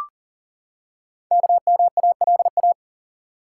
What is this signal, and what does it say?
Morse code, heard as a keyed tone, about 44 wpm, 700 Hz. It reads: KMALA
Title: Code Groups - 5 Letters - 40wpm